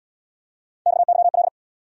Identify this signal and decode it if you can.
Morse code, heard as a keyed tone, about 69 wpm, 700 Hz. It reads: XYL